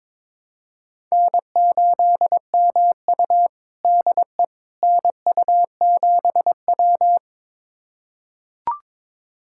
Morse code, heard as a keyed tone, about 22 wpm, 700 Hz. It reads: N8MU DE NU7W
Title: USA Callsign DE Callsign - 4 Characters - 22wpm